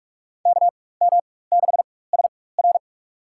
Morse code, heard as a keyed tone, about 44 wpm, spaced as for 21 wpm, 700 Hz. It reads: KM6SR
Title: USA Call Signs - 5 Characters - 4x Character Spacing - 45wpm